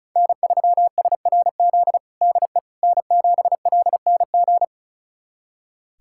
Morse code, heard as a keyed tone, about 36 wpm, 700 Hz. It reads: N3SRZ DE N7LNG
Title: USA Callsign DE Callsign - 5 Characters - 35wpm